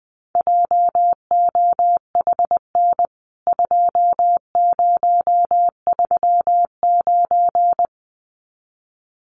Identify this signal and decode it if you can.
Morse code, heard as a keyed tone, about 20 wpm, 700 Hz. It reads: JOHN 2039
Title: CWT Exchange 20wpm